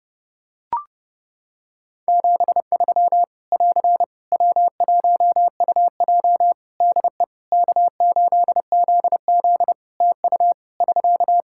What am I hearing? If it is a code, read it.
Morse code, heard as a keyed tone, about 30 wpm, 700 Hz. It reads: 73 <AR> W1UJ DE K8ZZ TU <SK>
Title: Practice QSO - Exchange 4 - Ending - 30wpm